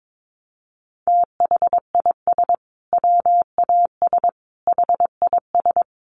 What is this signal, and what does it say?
Morse code, heard as a keyed tone, about 22 wpm, 700 Hz. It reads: THIS WAS HIS
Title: Sentences from Top 1500 Words 22wpm